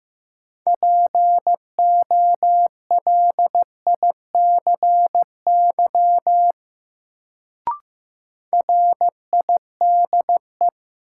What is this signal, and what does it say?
Morse code, heard as a keyed tone, about 15 wpm, 700 Hz. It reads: POLICY RIDE
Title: Sentences from Top 1200 Words - New Word Drill Alt - 15wpm